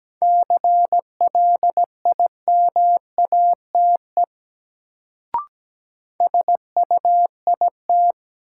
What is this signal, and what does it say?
Morse code, heard as a keyed tone, about 17 wpm, 700 Hz. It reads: CLIMATE SUIT
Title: Sentences from Top 1600 Words - New Word Drill Alt - 17wpm